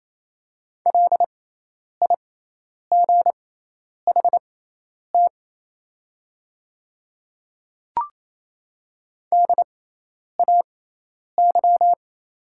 Morse code, heard as a keyed tone, about 28 wpm, spaced as for 10 wpm, 700 Hz. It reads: LIGHT DAY
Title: 6x Character Spacing - Top 200 Words - 28wpm